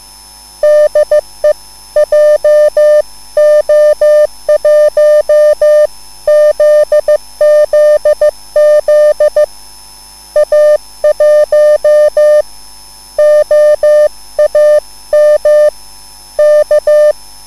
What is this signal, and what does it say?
Morse code, heard as a keyed tone, about 15 wpm, spaced as for 13 wpm, 600 Hz. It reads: DE JO1ZZZ A1OAM K
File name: cwbook